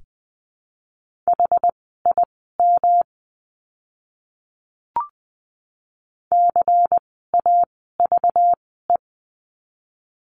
Morse code, heard as a keyed tone, about 20 wpm, spaced as for 15 wpm, 700 Hz. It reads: HIM CAVE
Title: 192 Lesson 31f - Letter v - Words 20wpm